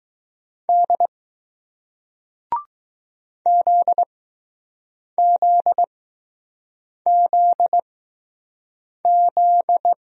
Morse code, heard as a keyed tone, about 20 wpm, spaced as for 6 wpm, 700 Hz. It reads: D ZZZZ